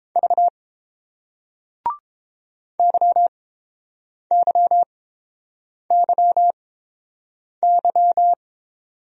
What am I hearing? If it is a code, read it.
Morse code, heard as a keyed tone, about 30 wpm, spaced as for 7 wpm, 690 Hz. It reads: V YYYY